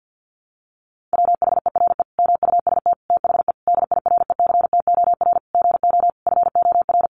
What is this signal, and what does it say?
Morse code, heard as a keyed tone, about 50 wpm, 700 Hz. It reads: WHERE MUST THE DIRECTOR GO FOR
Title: Sentences from Top 1300 Words 50wpm